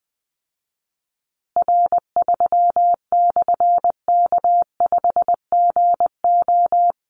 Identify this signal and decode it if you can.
Morse code, heard as a keyed tone, about 20 wpm, 700 Hz. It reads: R3/K5GO
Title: Call Signs Repeated 3x - 20wpm